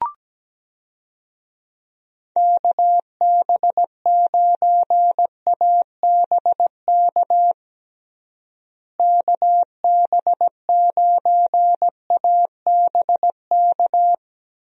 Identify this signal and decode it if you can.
Morse code, heard as a keyed tone, about 17 wpm, 700 Hz. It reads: KB9ABK KB9ABK
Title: USA Call Signs - 6 Characters - Repeated 3x - 17wpm